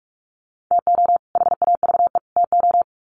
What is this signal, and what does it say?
Morse code, heard as a keyed tone, about 44 wpm, 700 Hz. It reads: TO HAVE TO
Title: Sets of 3 Words 45wpm